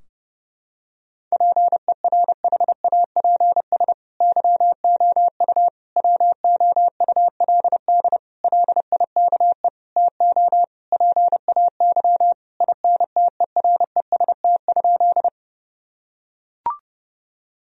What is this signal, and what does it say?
Morse code, heard as a keyed tone, about 30 wpm, 700 Hz. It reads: PERHAPS YOU WOULD LIKE TO PAY INTEREST?
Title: Sentences from Top 1000 Words - Review Part 2 - 30wpm